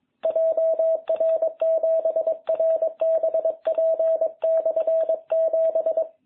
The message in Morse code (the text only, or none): JR7RBP/7